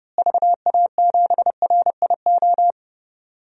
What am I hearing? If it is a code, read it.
Morse code, heard as a keyed tone, about 30 wpm, 700 Hz. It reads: VA7RIO
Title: DXCC-100 Call Signs - 6 Characters - Repeated 2x - 30wpm